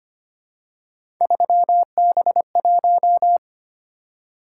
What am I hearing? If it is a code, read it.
Morse code, heard as a keyed tone, about 25 wpm, 700 Hz. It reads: 3B1
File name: Three Letter-Number Combinations - 25wpm